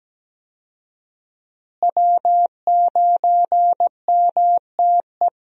Morse code, heard as a keyed tone, about 17 wpm, 700 Hz. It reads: W9MTE
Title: USA Call Signs - 5 Characters - Repeated 2x - 17wpm